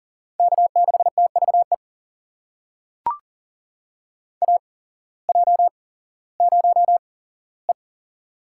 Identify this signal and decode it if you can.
Morse code, heard as a keyed tone, about 37 wpm, spaced as for 18 wpm, 700 Hz. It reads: K6TVE AJ0E
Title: USA Call Signs - 8x Character Spacing - 40wpm